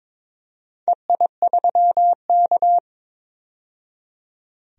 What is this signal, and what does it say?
Morse code, heard as a keyed tone, about 22 wpm, 700 Hz. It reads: EI3K
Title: DXCC-100 Call Signs - 4 Characters - 22wpm